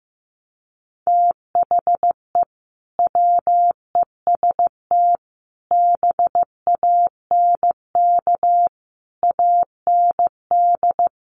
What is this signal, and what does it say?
Morse code, heard as a keyed tone, about 15 wpm, 700 Hz. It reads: THE WEST BANK AND
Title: Sets of 5 Words 15wpm